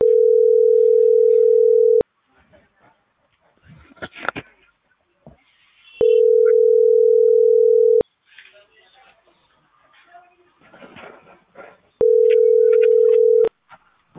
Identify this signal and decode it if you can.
Ringback tone (North America), 440 + 480 Hz.